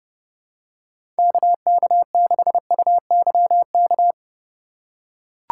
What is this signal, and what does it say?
Morse code, heard as a keyed tone, about 30 wpm, 700 Hz. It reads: KK6UYK
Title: USA Call Signs - 6 Characters - 3x Character Spacing - 30wpm